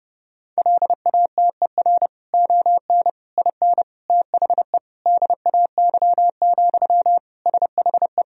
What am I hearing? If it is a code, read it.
Morse code, heard as a keyed tone, about 30 wpm, 700 Hz. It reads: LATER ON IN THE DAY, SHE
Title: Sentences from Top 1000 Words - Review Part 4 - 30wpm